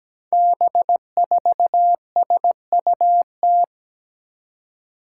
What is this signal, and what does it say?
Morse code, heard as a keyed tone, about 17 wpm, 700 Hz. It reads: B4SUT
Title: Five Challenging Letter-Number Combinations 17wpm